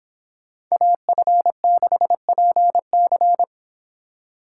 Morse code, heard as a keyed tone, about 29 wpm, 700 Hz. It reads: AF6PC